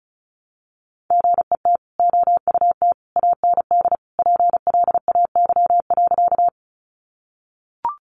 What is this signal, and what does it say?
Morse code, heard as a keyed tone, about 34 wpm, 700 Hz. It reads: GET OUT AND PLAY.
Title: Sentences from Top 200 Words 35wpm